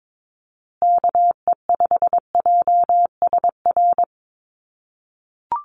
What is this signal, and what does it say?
Morse code, heard as a keyed tone, about 22 wpm, 700 Hz. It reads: KE5JSR
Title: USA Call Signs Repeated 3x 22wpm